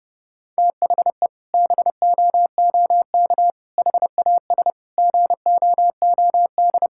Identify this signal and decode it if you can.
Morse code, heard as a keyed tone, about 30 wpm, 700 Hz. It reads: THE BOOK HAS GOOD
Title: Sentences from Top 1000 Words - Review Part 6 - 30wpm